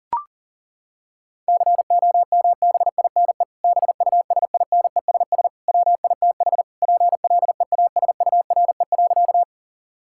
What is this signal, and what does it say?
Morse code, heard as a keyed tone, about 40 wpm, 700 Hz. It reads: COMBINE BUSINESS WITH PLEASURE.
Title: Sentences from Top 1600 Words 40wpm